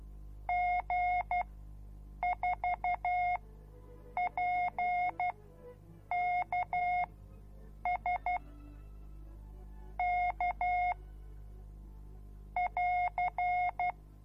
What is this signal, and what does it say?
Morse code, heard as a keyed tone, about 12 wpm, spaced as for 7 wpm, 720 Hz. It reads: G4PKS K <AR>